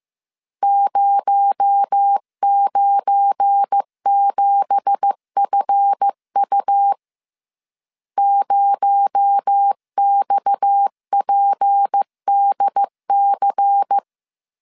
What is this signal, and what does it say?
Morse code, heard as a keyed tone, about 15 wpm, 800 Hz. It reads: 097FU 0XPDC
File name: CW / DET / 1 9 1500C S A 14wpm 3222